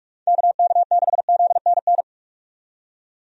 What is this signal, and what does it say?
Morse code, heard as a keyed tone, about 45 wpm, 700 Hz. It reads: KK6ZNN
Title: USA Call Signs - 6 Characters - 45wpm